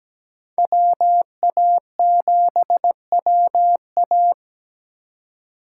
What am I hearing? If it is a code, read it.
Morse code, heard as a keyed tone, about 17 wpm, 700 Hz. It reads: WA7WA
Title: USA Call Signs - 5 Characters - 3x Character Spacing - 17wpm